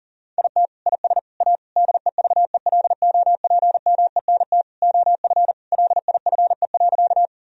Morse code, heard as a keyed tone, about 40 wpm, 700 Hz. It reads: IT IS A DEVELOPMENT OF LIFE.
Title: Sentences from Top 1000 Words - Review Part 4 - 40wpm